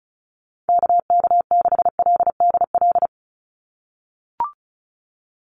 Morse code, heard as a keyed tone, about 34 wpm, 700 Hz. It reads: KK6LDL
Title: USA Call Signs 35wpm